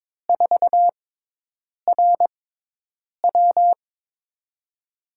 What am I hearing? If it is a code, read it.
Morse code, heard as a keyed tone, about 21 wpm, spaced as for 8 wpm, 700 Hz. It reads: 4RW
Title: USA Call Signs - 5 Characters - 6x Character Spacing - 22wpm